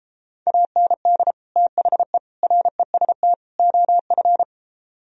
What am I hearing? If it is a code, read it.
Morse code, heard as a keyed tone, about 29 wpm, 700 Hz. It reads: AND THE REST OF